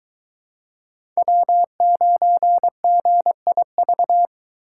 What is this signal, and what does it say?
Morse code, heard as a keyed tone, about 23 wpm, 700 Hz. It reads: W9GIV